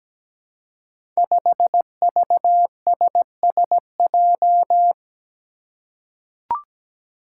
Morse code, heard as a keyed tone, about 17 wpm, 700 Hz. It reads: 5VSSJ
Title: Five Challenging Letter-Number Combinations 17wpm